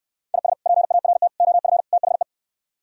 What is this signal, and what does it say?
Morse code, heard as a keyed tone, about 68 wpm, 700 Hz. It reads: IS PART OF THE